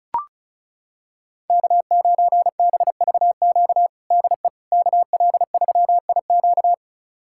Morse code, heard as a keyed tone, about 36 wpm, 700 Hz. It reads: K9BVQ DE KL3IQ
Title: USA Callsign DE Callsign - 5 Characters - 35wpm